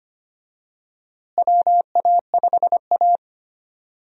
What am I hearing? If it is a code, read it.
Morse code, heard as a keyed tone, about 25 wpm, 700 Hz. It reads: WA5A
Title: USA Call Signs - 4 Characters - Repeated 2x - 25wpm